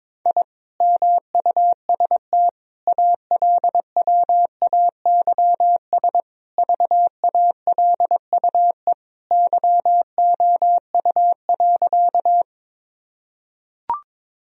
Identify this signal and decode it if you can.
Morse code, heard as a keyed tone, about 22 wpm, 700 Hz. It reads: I MUST ALWAYS VALUE YOU.
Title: Sentences from Top 1000 Words - Review Part 3 - 22wpm